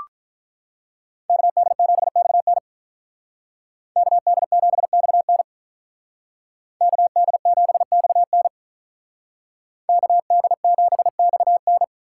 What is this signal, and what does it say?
Morse code, heard as a keyed tone, about 43 wpm, 700 Hz. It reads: KD7XN KD7XN KD7XN KD7XN
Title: Speed Racing - USA 2x2 Call Signs 35wpm